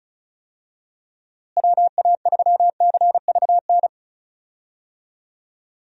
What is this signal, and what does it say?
Morse code, heard as a keyed tone, about 34 wpm, 700 Hz. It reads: WA3CVN
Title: USA Call Signs - 6 Characters - Repeated 3x - 35wpm